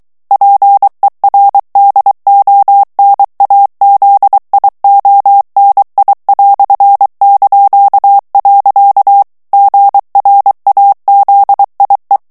Morse code, heard as a keyed tone, about 23 wpm, 800 Hz. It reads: PERDONAZIONI"). GRAZIE
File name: morse mp3 / ITALIEN texte clair 1 vitesse 22 mots minute